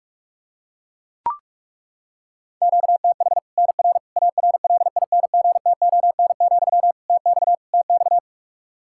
Morse code, heard as a keyed tone, about 41 wpm, 700 Hz. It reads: QTH NR ARLINGTON, TX TX
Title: K1USN SST QSO Practice 45wpm